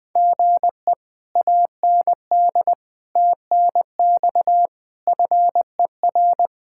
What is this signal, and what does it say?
Morse code, heard as a keyed tone, about 20 wpm, 700 Hz. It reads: GE AND TNX FER
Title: Speed-Racing Common QSO Phrases Encore - 17wpm